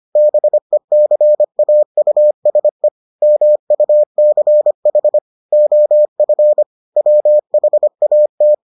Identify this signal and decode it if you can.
Morse code, heard as a keyed tone, about 25 wpm, 600 Hz. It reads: BECAUSE MUCH OF WHAT